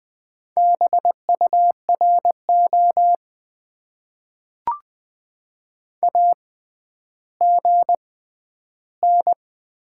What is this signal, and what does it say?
Morse code, heard as a keyed tone, about 20 wpm, spaced as for 11 wpm, 700 Hz. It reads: BURO AGN
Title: Top 90 QSO Elements - 6x Character Spacing - 20wpm